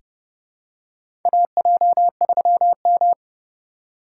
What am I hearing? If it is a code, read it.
Morse code, heard as a keyed tone, about 31 wpm, 700 Hz. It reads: AJ3M
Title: USA Call Signs - 4 Characters - 2x Character Spacing - 30wpm